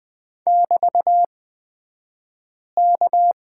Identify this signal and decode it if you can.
Morse code, heard as a keyed tone, about 20 wpm, spaced as for 5 wpm, 690 Hz. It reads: <BT>K
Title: 235 Lesson 38d - Letter q - Two Characters 20wpm